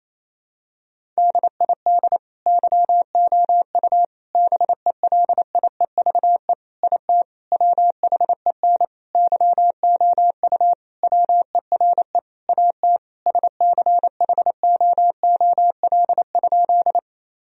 Morse code, heard as a keyed tone, about 28 wpm, 700 Hz. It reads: DID YOU BELIEVE IT WHEN YOU WERE AT SCHOOL?
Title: Sentences from Top 1000 Words - Review Part 1 - 28wpm